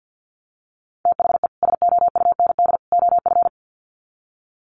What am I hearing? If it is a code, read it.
Morse code, heard as a keyed tone, about 50 wpm, 700 Hz. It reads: THE SOUND OF
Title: Sets of 3 Words 50wpm